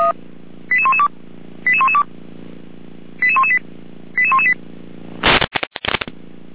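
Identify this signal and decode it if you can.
Five-tone selective call (ZVEI), heard as quick runs of five tones, tones of about 70 ms. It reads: 88112 88112 88118 88118
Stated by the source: ZVEI five-tone call 88191 received on SDR.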